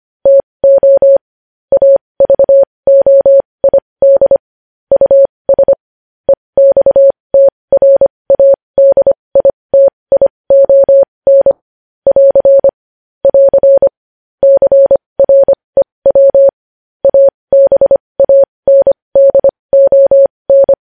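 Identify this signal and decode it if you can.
Morse code, heard as a keyed tone, about 25 wpm, spaced as for 20 wpm, 550 Hz. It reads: TO AVOID US EXTRADITION <AR> <AR> CREW ABANDON
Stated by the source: Morse code news headlines sent at 20.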